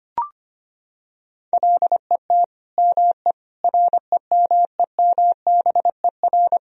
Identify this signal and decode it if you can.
Morse code, heard as a keyed tone, about 25 wpm, 700 Hz. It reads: LET ME REMEMBER
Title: Sentences from Top 400 Words and Abbreviations 25wpm